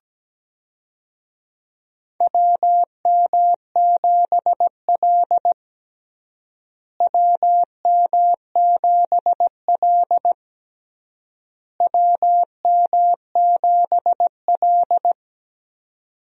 Morse code, heard as a keyed tone, about 17 wpm, 700 Hz. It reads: WM7L WM7L WM7L
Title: USA Call Signs - 4 Characters - Repeated 3x - 17wpm